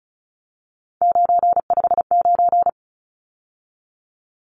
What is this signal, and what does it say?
Morse code, heard as a keyed tone, about 34 wpm, 700 Hz. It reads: 959
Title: Numbers - 3-Digit 35wpm